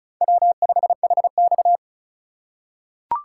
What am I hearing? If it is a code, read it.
Morse code, heard as a keyed tone, about 34 wpm, 700 Hz. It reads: W5HX
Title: USA Call Signs - 4 Characters - 4x Character Spacing - 35wpm